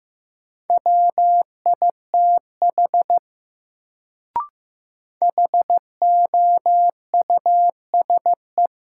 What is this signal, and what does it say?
Morse code, heard as a keyed tone, about 15 wpm, 700 Hz. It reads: WITH HOUSE
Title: Sets of 2 Words - New Word Drill - 15wpm